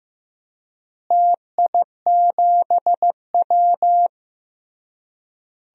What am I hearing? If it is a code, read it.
Morse code, heard as a keyed tone, about 16 wpm, 700 Hz. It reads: TI7W
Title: DXCC-100 Call Signs - 4 Characters - Repeated 3x - 15wpm